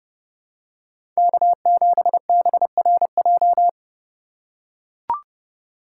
Morse code, heard as a keyed tone, about 30 wpm, 700 Hz. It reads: K7BRJ
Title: USA Call Signs - 2x Character Spacing - 30wpm